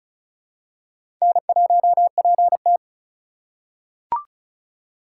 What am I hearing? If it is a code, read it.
Morse code, heard as a keyed tone, about 36 wpm, 700 Hz. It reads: N1PT
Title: USA Call Signs - 4x Character Spacing - 35wpm